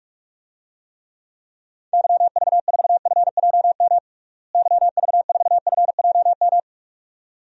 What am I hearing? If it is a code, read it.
Morse code, heard as a keyed tone, about 46 wpm, 700 Hz. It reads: YV4FJM YV4FJM
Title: DXCC-100 Call Signs - 6 Characters - Repeated 2x - 45wpm